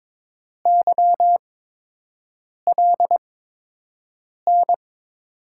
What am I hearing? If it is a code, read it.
Morse code, heard as a keyed tone, about 22 wpm, spaced as for 6 wpm, 700 Hz. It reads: YLN
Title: USA Call Signs - 6 Characters - 8x Character Spacing - 22wpm